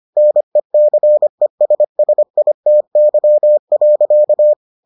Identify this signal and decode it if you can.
Morse code, heard as a keyed tone, about 25 wpm, 600 Hz. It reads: NECESSITY.